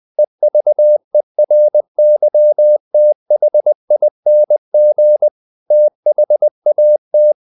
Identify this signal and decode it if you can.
Morse code, heard as a keyed tone, about 20 wpm, 600 Hz. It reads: EVERYTHING THAT
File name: the old man and the sea / the old man and the sea part 1 20wpm